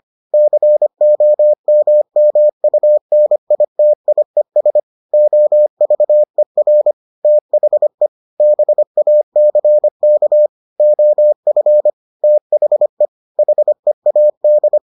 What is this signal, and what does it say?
Morse code, heard as a keyed tone, about 25 wpm, 600 Hz. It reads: COMMUNITIES OVER THE BACK OF THE HEAD